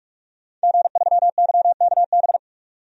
Morse code, heard as a keyed tone, about 46 wpm, 700 Hz. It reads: G3YKB
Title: DXCC-100 Call Signs - 5 Characters - 45wpm